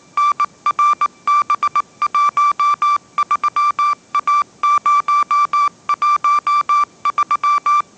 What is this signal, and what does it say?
Morse code, heard as a keyed tone, about 19 wpm, 1170 Hz. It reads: NRB13A013
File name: 20wpm